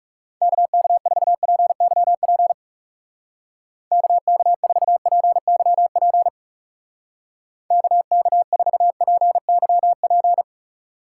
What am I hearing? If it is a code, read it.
Morse code, heard as a keyed tone, about 36 wpm, 700 Hz. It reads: KK4PYP KK4PYP KK4PYP
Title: Speed Racing - USA Call Signs 30wpm